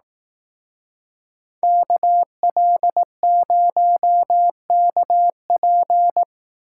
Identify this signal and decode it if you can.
Morse code, heard as a keyed tone, about 19 wpm, 700 Hz. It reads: KL0KP
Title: Speed Racing - USA 2x2 Call Signs 15wpm